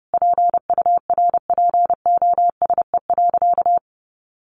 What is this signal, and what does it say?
Morse code, heard as a keyed tone, about 30 wpm, 700 Hz. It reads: PURPOSE.
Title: Sentences from Top 1400 Words 30wpm